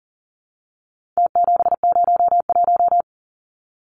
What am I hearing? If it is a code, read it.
Morse code, heard as a keyed tone, about 40 wpm, 700 Hz. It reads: T701
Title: Numbers - Contest Style - 4-Digit 40wpm